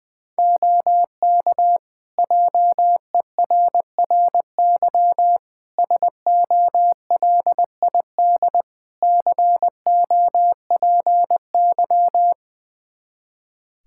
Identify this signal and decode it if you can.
Morse code, heard as a keyed tone, about 20 wpm, 700 Hz. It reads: OK JERRY SOLID COPY
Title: Practice QSO - Exchange 3 - More Info - 20wpm